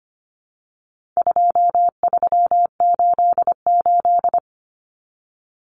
Morse code, heard as a keyed tone, about 25 wpm, 700 Hz. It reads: 2388
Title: Numbers - 4-Digit 25wpm